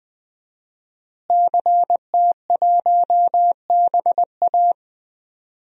Morse code, heard as a keyed tone, about 20 wpm, 700 Hz. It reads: CT1BA
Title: DXCC-100 Call Signs - 5 Characters - Repeated 3x - 20wpm